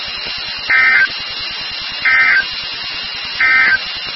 Emergency Alert System (SAME) tones, heard as harsh buzzing bursts. It is heard from 0.6 to 3.8 s.
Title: NOAA Weather Radio WXK36